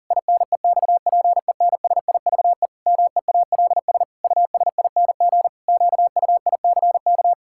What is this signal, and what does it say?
Morse code, heard as a keyed tone, about 40 wpm, 700 Hz. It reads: INEXPENSIVE MEALS USING QUICK
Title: Pangram Challenge v2 - 40wpm